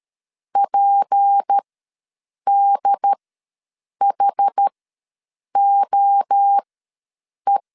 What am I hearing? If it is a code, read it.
Morse code, heard as a keyed tone, about 13 wpm, spaced as for 7 wpm, 800 Hz. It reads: PDHOE
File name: LEZIONI / 6 PKX0 / 6 P ws 12wpm 3117